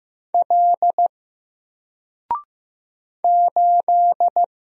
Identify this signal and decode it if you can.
Morse code, heard as a keyed tone, about 15 wpm, 700 Hz. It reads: L 8
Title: Single Character (Letter:Number:Punctuation:Prosign) 15wpm